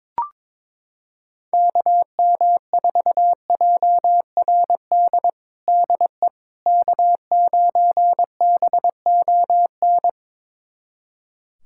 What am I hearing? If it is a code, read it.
Morse code, heard as a keyed tone, about 22 wpm, 700 Hz. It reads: KM4JRD DE K9BON
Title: USA Callsign DE Callsign 22wpm